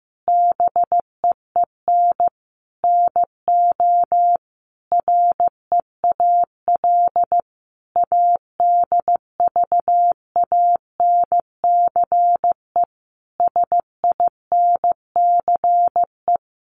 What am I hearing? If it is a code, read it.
Morse code, heard as a keyed tone, about 15 wpm, 700 Hz. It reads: BEEN NO REAL ADVANCE SINCE
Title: Sentences from Top 1400 Words 15wpm